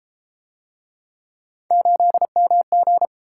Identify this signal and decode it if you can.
Morse code, heard as a keyed tone, about 33 wpm, 700 Hz. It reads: 8MG